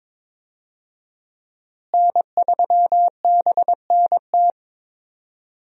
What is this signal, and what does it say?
Morse code, heard as a keyed tone, about 22 wpm, 700 Hz. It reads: N3BNT
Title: USA Call Signs - 5 Characters - Repeated 2x - 22wpm